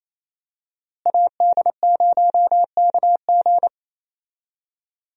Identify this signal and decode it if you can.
Morse code, heard as a keyed tone, about 28 wpm, 700 Hz. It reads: AD0KG
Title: USA Call Signs - 5 Characters - Repeated 3x - 28wpm